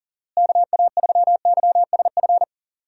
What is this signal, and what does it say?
Morse code, heard as a keyed tone, about 40 wpm, 700 Hz. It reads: KA3YSF
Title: USA Call Signs - 6 Characters - Repeated 3x - 40wpm